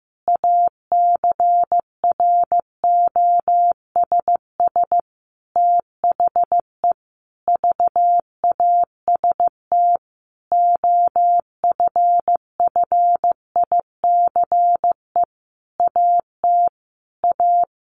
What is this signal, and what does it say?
Morse code, heard as a keyed tone, about 14 wpm, 700 Hz. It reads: ACROSS THE VAST OFFICE AT A
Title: A Sound Of Thunder 15wpm